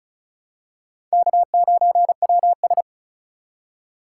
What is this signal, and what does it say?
Morse code, heard as a keyed tone, about 36 wpm, 700 Hz. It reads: K9WS
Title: USA Call Signs - 3x Character Spacing - 35wpm